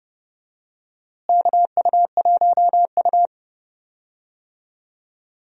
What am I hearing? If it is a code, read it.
Morse code, heard as a keyed tone, about 30 wpm, 700 Hz. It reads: KU1U
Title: USA Call Signs - 4 Characters - Repeated 2x - 30wpm